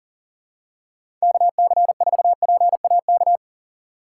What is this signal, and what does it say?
Morse code, heard as a keyed tone, about 40 wpm, 700 Hz. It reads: KC4PAK
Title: USA Call Signs - 6 Characters - 6x Character Spacing - 40wpm